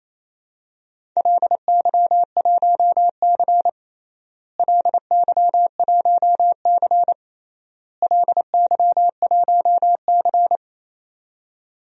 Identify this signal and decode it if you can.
Morse code, heard as a keyed tone, about 28 wpm, 700 Hz. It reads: LY1C LY1C LY1C
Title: DXCC-100 Call Signs - 4 Characters - Repeated 3x - 28wpm